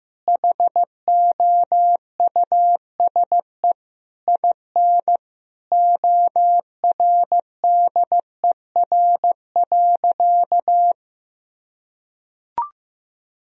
Morse code, heard as a keyed tone, about 15 wpm, 700 Hz. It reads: HOUSE IN ORDER.
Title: Sentences from Top 1000 Words - Review Part 2 - 15wpm